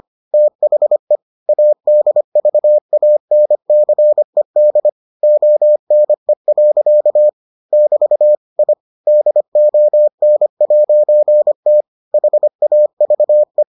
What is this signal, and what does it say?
Morse code, heard as a keyed tone, about 25 wpm, 600 Hz. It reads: THE ADVANCED ONE. <BT> I DON'T HAVE